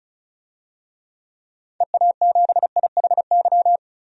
Morse code, heard as a keyed tone, about 34 wpm, 700 Hz. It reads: EA7IHY
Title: DXCC-100 Call Signs - 6 Characters - 35wpm